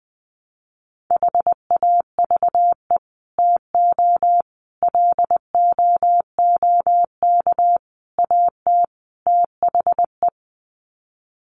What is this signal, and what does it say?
Morse code, heard as a keyed tone, about 20 wpm, 700 Hz. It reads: HAVE TO LOOK AT THE
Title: Sets of 5 Words Encore 20wpm